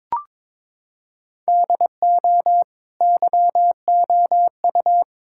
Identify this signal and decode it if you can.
Morse code, heard as a keyed tone, about 22 wpm, 700 Hz. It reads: DO YOU
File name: practice / 22wpm / Sentences with 2-Word Phrases - 22wpm